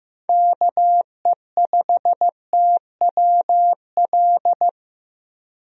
Morse code, heard as a keyed tone, about 16 wpm, 700 Hz. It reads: KE5TWL
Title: USA Call Signs - 6 Characters - Repeated 3x - 15wpm